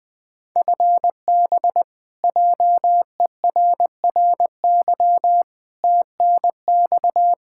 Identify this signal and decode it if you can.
Morse code, heard as a keyed tone, about 20 wpm, 700 Hz. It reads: FB JERRY TNX
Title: K1USN SST QSO Practice 20wpm